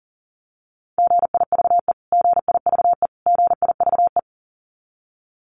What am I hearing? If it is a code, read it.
Morse code, heard as a keyed tone, about 40 wpm, 700 Hz. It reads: GIVE GIVE GIVE
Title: Top 200 Words Repeated 40wpm